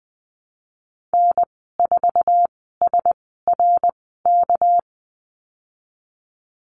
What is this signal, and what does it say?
Morse code, heard as a keyed tone, about 20 wpm, spaced as for 15 wpm, 700 Hz. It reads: N4SRK
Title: 219 Lesson 35g - Number 0 - Call Signs 20wpm